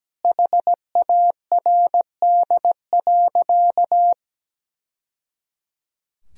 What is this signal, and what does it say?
Morse code, heard as a keyed tone, about 17 wpm, 700 Hz. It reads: HARD.
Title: Sentences from Top 1000 Words - Review Part 4 - 17wpm